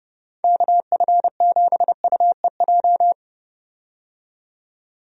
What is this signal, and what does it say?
Morse code, heard as a keyed tone, about 32 wpm, 700 Hz. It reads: KF7UEJ
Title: USA Call Signs - 6 Characters - 30wpm